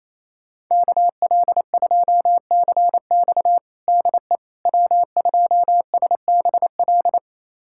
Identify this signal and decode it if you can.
Morse code, heard as a keyed tone, about 28 wpm, 700 Hz. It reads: KL2CX DE W2SBL
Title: USA Callsign DE Callsign - 5 Characters - 28wpm